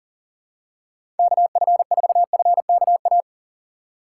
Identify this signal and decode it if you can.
Morse code, heard as a keyed tone, about 40 wpm, 700 Hz. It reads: KF4FKA
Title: USA Call Signs - 6 Characters - 4x Character Spacing - 40wpm